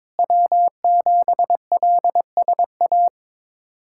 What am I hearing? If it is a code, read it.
Morse code, heard as a keyed tone, about 22 wpm, 700 Hz. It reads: W7LSA
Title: USA Callsign DE Callsign 22wpm